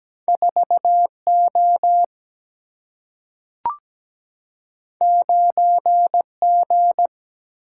Morse code, heard as a keyed tone, about 17 wpm, 700 Hz. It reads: 4O 9G
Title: Single-Letter Single-Number 17wpm